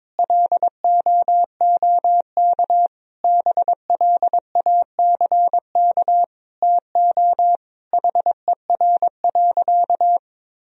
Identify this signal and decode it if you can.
Morse code, heard as a keyed tone, about 22 wpm, 700 Hz. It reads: LOOK BLACK TO HER.
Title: Sentences from Top 1000 Words - Review Part 2 - 22wpm